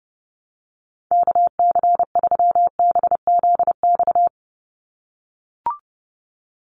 Morse code, heard as a keyed tone, about 30 wpm, 700 Hz. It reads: KC3BZX
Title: USA Call Signs 30wpm